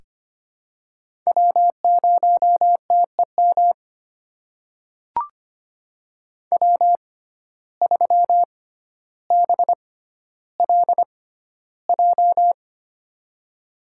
Morse code, heard as a keyed tone, about 25 wpm, spaced as for 13 wpm, 700 Hz. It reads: W0TEM W3BLJ
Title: USA Call Signs - 5 Characters - 6x Character Spacing - 25wpm